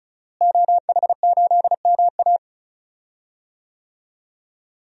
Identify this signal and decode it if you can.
Morse code, heard as a keyed tone, about 36 wpm, 700 Hz. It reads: OH8MA
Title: DXCC-100 Call Signs - 5 Characters - Repeated 3x - 35wpm